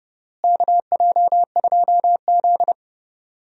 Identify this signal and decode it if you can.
Morse code, heard as a keyed tone, about 30 wpm, 700 Hz. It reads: KJ2Z
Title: USA Call Signs - 4 Characters - Repeated 2x - 30wpm